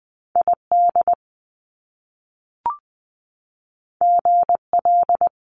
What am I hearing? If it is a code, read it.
Morse code, heard as a keyed tone, about 20 wpm, 700 Hz. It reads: I D G L
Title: Top Two Letter Combinations in Words 20wpm